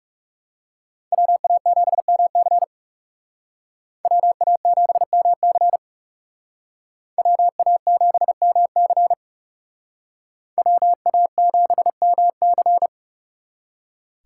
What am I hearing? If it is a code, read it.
Morse code, heard as a keyed tone, about 38 wpm, 700 Hz. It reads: WA7MC WA7MC WA7MC WA7MC